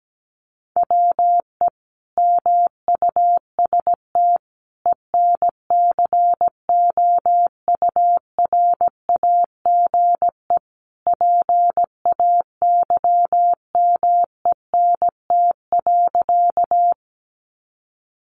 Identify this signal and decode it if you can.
Morse code, heard as a keyed tone, about 17 wpm, 700 Hz. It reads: WE MUST ENCOURAGE PAYMENT.
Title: Sentences from Top 1400 Words 17wpm